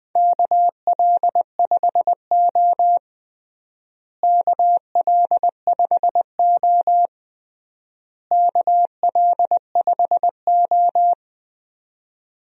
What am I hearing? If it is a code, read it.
Morse code, heard as a keyed tone, about 19 wpm, 700 Hz. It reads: KL5O KL5O KL5O
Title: USA Call Signs - 4 Characters - Repeated 3x - 20wpm